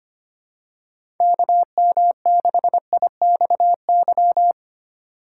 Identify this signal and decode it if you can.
Morse code, heard as a keyed tone, about 25 wpm, 700 Hz. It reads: KM6IXY
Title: USA Call Signs - 2x Character Spacing - 25wpm